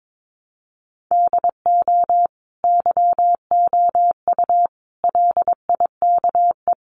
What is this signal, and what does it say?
Morse code, heard as a keyed tone, about 22 wpm, 700 Hz. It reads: DO YOU LIKE
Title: Sentences from Top 1200 Words 22wpm